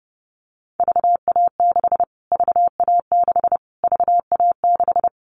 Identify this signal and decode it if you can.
Morse code, heard as a keyed tone, about 28 wpm, 690 Hz. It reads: VA6 VA6 VA6
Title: Callsign Prefixes for Top 25 DXCC - Repeated 3x - 30wpm